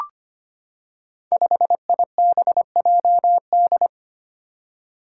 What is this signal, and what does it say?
Morse code, heard as a keyed tone, about 25 wpm, 700 Hz. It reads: 5IBJD
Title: Five Challenging Letter-Number Combinations 25wpm